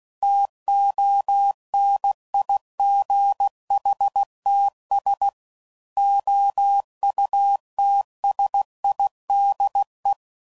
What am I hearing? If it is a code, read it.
Morse code, heard as a keyed tone, about 16 wpm, 800 Hz. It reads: TONIGHTS OUTSIDE
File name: CW Code Groups @15WPM / Weather Sets / Weather 5 22